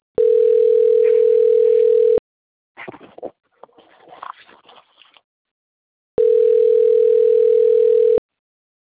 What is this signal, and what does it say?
Ringback tone (North America), 440 + 480 Hz.
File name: out- - - - - .79262